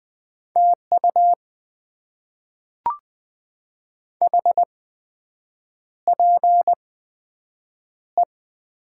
Morse code, heard as a keyed tone, about 20 wpm, spaced as for 7 wpm, 700 Hz. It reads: TU HPE
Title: Top 90 QSO Elements - 8x Character Spacing - 20wpm